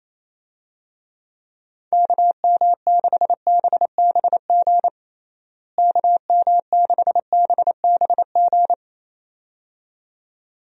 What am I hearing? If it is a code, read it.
Morse code, heard as a keyed tone, about 27 wpm, 700 Hz. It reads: KM6BBG KM6BBG
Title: USA Call Signs - 6 Characters - Repeated 2x - 28wpm